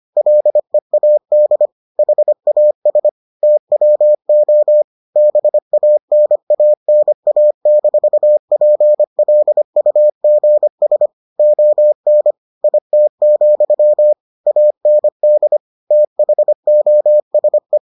Morse code, heard as a keyed tone, about 25 wpm, 600 Hz. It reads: LEAD HAS TWO BANANA-PLUGS ON IT, AND THOSE